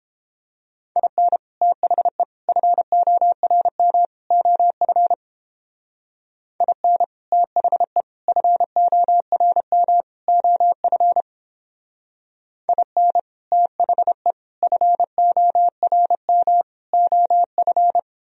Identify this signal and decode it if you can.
Morse code, heard as a keyed tone, about 29 wpm, 700 Hz. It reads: IN THE FORM OF IN THE FORM OF IN THE FORM OF